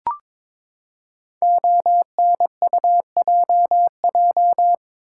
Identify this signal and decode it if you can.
Morse code, heard as a keyed tone, about 22 wpm, 700 Hz. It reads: ONUJJ
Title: Code Groups - 5 Letters - 22wpm